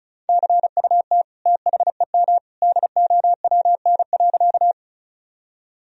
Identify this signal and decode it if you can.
Morse code, heard as a keyed tone, about 36 wpm, 700 Hz. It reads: CUT THEM DOWN.
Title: Sentences from Top 1000 Words - Review Part 1 - 35wpm